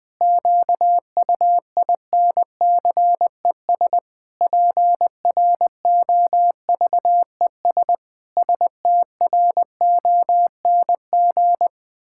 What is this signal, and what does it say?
Morse code, heard as a keyed tone, about 21 wpm, 700 Hz. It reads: QUINCES PROVES STRONG
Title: Pangram Challenge v2 - 20wpm